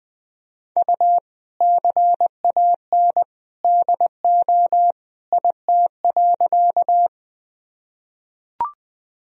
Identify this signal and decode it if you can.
Morse code, heard as a keyed tone, about 20 wpm, 700 Hz. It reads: U CAN DO IT.
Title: Sentences from Top 100 Words and Abbreviations 20wpm